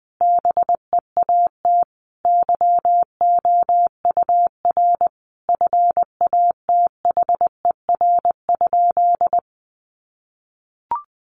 Morse code, heard as a keyed tone, about 20 wpm, 700 Hz. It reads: BEAT YOUR FATHER?
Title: Sentences from Top 1300 Words 20wpm